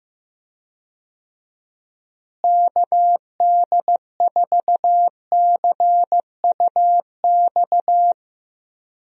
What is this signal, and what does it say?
Morse code, heard as a keyed tone, about 15 wpm, 700 Hz. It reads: KD4CUX
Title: USA Call Signs - 6 Characters - Repeated 2x - 15wpm